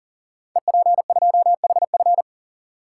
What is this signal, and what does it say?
Morse code, heard as a keyed tone, about 41 wpm, 700 Hz. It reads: EP2HF
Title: Code Groups - 5 Letter:Numbers - 40wpm